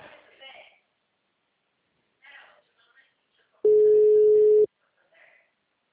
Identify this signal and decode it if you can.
Ringback tone (Europe, 425 Hz), with speech.